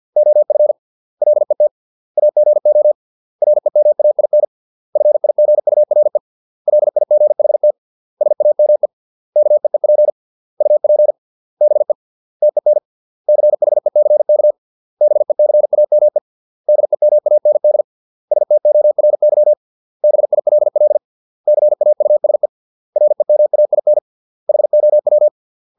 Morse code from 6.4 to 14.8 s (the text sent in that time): LIGHT SAME KEEP UP BE TEN CHECK